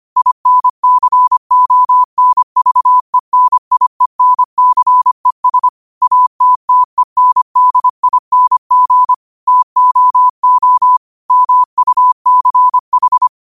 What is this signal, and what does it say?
Morse code, heard as a keyed tone, about 25 wpm, 1000 Hz. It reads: INCONVENIENCES ATTENDING TOO MUCH
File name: qotdfiles / 25wpm